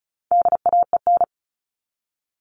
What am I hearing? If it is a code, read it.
Morse code, heard as a keyed tone, about 34 wpm, 700 Hz. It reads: DAEN
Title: All Cut Numbers - 4-Digit 35wpm